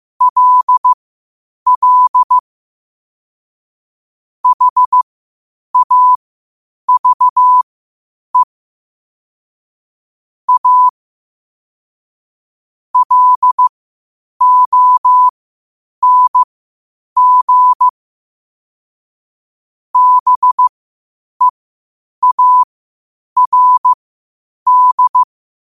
Morse code, heard as a keyed tone, about 15 wpm, spaced as for 8 wpm, 1000 Hz. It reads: LL HAVE A LONG BEARD
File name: qotdfiles / 8wpm